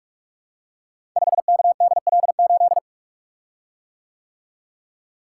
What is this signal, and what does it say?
Morse code, heard as a keyed tone, about 44 wpm, 700 Hz. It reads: 5KDL8
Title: Code Groups - 5 Letter:Numbers - 45wpm